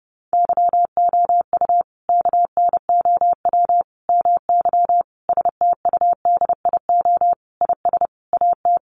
Morse code, heard as a keyed tone, about 30 wpm, 700 Hz. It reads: YOU KNOW MY STUDIO IS AT
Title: Sentences from Top 1200 Words 30wpm